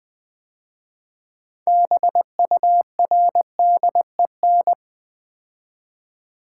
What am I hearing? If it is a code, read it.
Morse code, heard as a keyed tone, about 20 wpm, 700 Hz. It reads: BURDEN
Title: 6 Letter Words - First 3 Letters Given - 20wpm